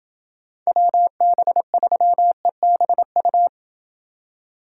Morse code, heard as a keyed tone, about 27 wpm, 700 Hz. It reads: WB3EBU